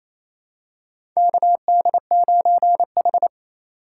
Morse code, heard as a keyed tone, about 28 wpm, 700 Hz. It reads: KD9H
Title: USA Call Signs - 3x Character Spacing - 28wpm